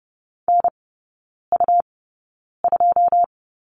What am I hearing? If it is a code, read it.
Morse code, heard as a keyed tone, about 30 wpm, spaced as for 9 wpm, 700 Hz. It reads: NU2
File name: Lesson 17e - Letter u - Three Characters 30wpm